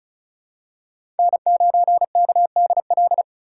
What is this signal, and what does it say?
Morse code, heard as a keyed tone, about 34 wpm, 700 Hz. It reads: N9KDL
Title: USA Call Signs - 5 Characters - 3x Character Spacing - 35wpm